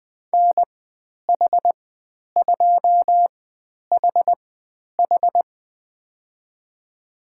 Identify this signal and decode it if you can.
Morse code, heard as a keyed tone, about 20 wpm, spaced as for 10 wpm, 700 Hz. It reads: NH2HH
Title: 095 Lesson 16g - Letter c - Call Signs 20wpm